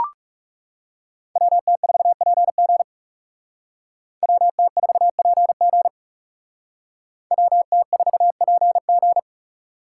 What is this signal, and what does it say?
Morse code, heard as a keyed tone, about 38 wpm, 700 Hz. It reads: WT4PG WT4PG WT4PG